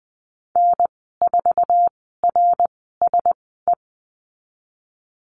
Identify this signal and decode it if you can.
Morse code, heard as a keyed tone, about 19 wpm, spaced as for 14 wpm, 700 Hz. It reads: N4RSE
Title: 055 Lesson 10g - Letter r - Call Signs 20wpm